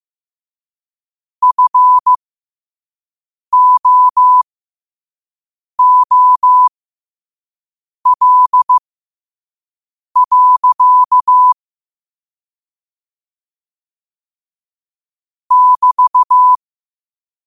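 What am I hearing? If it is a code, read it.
Morse code, heard as a keyed tone, about 15 wpm, spaced as for 5 wpm, 1000 Hz. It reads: FOOL. <BT>
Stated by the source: Morse code quote of the day at 5 wpm.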